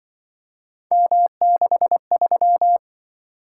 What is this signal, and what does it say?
Morse code, heard as a keyed tone, about 25 wpm, 700 Hz. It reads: M63